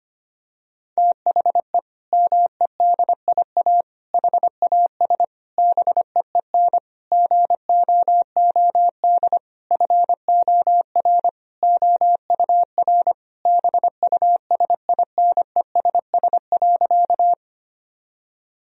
Morse code, heard as a keyed tone, about 25 wpm, 700 Hz. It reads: THE MEDIA HAS BEEN GOOD FOR OUR BUSINESS.
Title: Sentences from Top 1000 Words - Review Part 4 - 25wpm